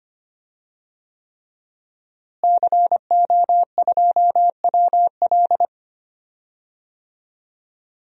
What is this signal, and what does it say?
Morse code, heard as a keyed tone, about 25 wpm, 700 Hz. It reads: CO2WL
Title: DXCC-100 Call Signs - 5 Characters - Repeated 3x - 25wpm